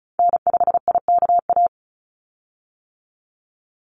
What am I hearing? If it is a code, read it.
Morse code, heard as a keyed tone, about 34 wpm, 700 Hz. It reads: N5IKA
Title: USA Call Signs 35wpm